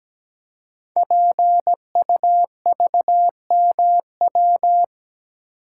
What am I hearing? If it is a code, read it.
Morse code, heard as a keyed tone, about 17 wpm, 700 Hz. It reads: PUVMW
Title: Code Groups - 5 Letters - 17wpm